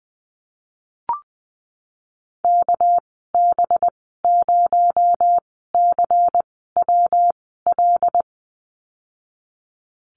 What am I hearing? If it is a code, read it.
Morse code, heard as a keyed tone, about 20 wpm, spaced as for 15 wpm, 700 Hz. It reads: KB0CWL
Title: 238 Lesson 38g - Letter q - Call Signs 20wpm